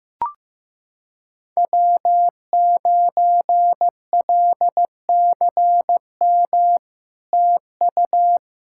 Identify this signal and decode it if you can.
Morse code, heard as a keyed tone, about 15 wpm, 700 Hz. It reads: W9LCM TU
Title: SOTA QSO Practice 15wpm